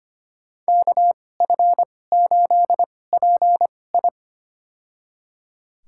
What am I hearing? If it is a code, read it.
Morse code, heard as a keyed tone, about 25 wpm, spaced as for 18 wpm, 700 Hz. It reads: KF8PI
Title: USA Call Signs - 5 Characters - 2x Character Spacing - 25wpm